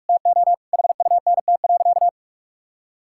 Morse code, heard as a keyed tone, about 46 wpm, 700 Hz. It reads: TO HUNT.
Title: Sentences from Top 1600 Words 45wpm